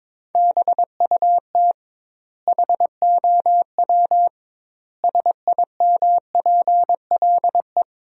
Morse code, heard as a keyed tone, about 21 wpm, 700 Hz. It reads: BUT HOW SIMPLE
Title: 2x Word-Spacing - Sentences from Top 400 Words - 22wpm